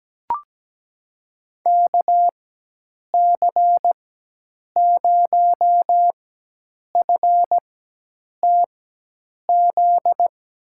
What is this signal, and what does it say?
Morse code, heard as a keyed tone, about 17 wpm, spaced as for 8 wpm, 700 Hz. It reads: KC0FTZ
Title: USA Call Signs - 6 Characters - 4x Character Spacing - 17wpm